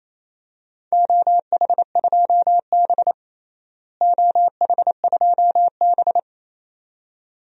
Morse code, heard as a keyed tone, about 28 wpm, 700 Hz. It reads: OH2B OH2B
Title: DXCC-100 Call Signs - 4 Characters - Repeated 2x - 28wpm